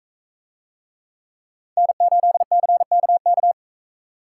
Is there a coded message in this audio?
Yes: Morse code from 1.8 to 3.5 s, heard as a keyed tone, about 40 wpm, 700 Hz. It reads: N8CKK